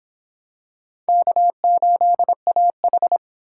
Morse code, heard as a keyed tone, about 25 wpm, 700 Hz. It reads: K8AH